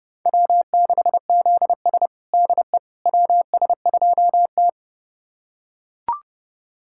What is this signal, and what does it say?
Morse code, heard as a keyed tone, about 30 wpm, 700 Hz. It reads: W6ZS DE WS2T
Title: USA Callsign DE Callsign - 4 Characters - 30wpm